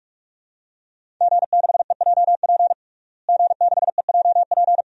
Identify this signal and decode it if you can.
Morse code, heard as a keyed tone, about 46 wpm, 700 Hz. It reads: G6EJP G6EJP
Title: DXCC-100 Call Signs - 5 Characters - Repeated 2x - 45wpm